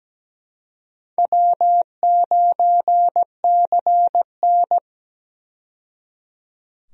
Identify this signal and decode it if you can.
Morse code, heard as a keyed tone, about 17 wpm, 700 Hz. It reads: W9CN
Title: USA Call Signs - 4 Characters - 17wpm